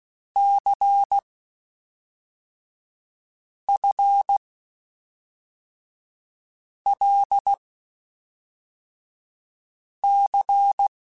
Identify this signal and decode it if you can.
Morse code, heard as a keyed tone, about 16 wpm, 800 Hz. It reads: C F L C
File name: Basic Sets / Basic Letter Groups Collective7 / LFCYQ@5